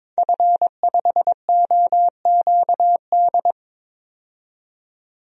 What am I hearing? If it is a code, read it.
Morse code, heard as a keyed tone, about 22 wpm, 700 Hz. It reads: F5OQD
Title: DXCC-100 Call Signs - 5 Characters - 22wpm